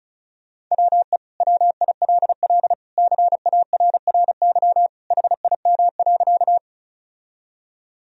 Morse code, heard as a keyed tone, about 34 wpm, 700 Hz. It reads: WE WILL CARRY HIM.
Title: Sentences from Top 1000 Words - Review Part 1 - 35wpm